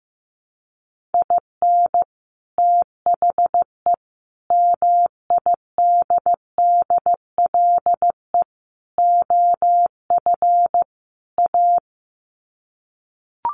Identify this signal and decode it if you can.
Morse code, heard as a keyed tone, about 15 wpm, 700 Hz. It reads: IN THE MIDDLE OF A
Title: Sets of 5 Words 15wpm